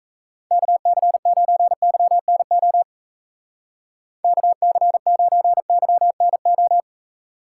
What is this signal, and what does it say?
Morse code, heard as a keyed tone, about 34 wpm, 700 Hz. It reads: KC9YNO KC9YNO
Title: Speed Racing - USA 2x3 Call Signs 28wpm